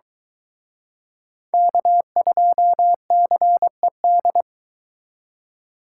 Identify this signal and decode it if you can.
Morse code, heard as a keyed tone, about 23 wpm, 700 Hz. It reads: K2CED